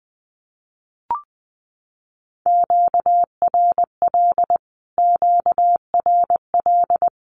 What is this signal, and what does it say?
Morse code, heard as a keyed tone, about 20 wpm, 700 Hz. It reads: QRL QRL
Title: Q-Codes Repeated 3x 20wpm